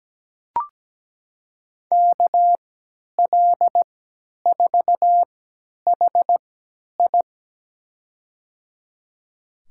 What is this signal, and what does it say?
Morse code, heard as a keyed tone, about 17 wpm, spaced as for 10 wpm, 700 Hz. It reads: KL4HI
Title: USA Call Signs - 5 Characters - 3x Character Spacing - 17wpm